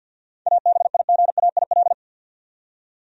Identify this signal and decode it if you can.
Morse code, heard as a keyed tone, about 46 wpm, 700 Hz. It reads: ABIGAIL
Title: USA Common Names - 50wpm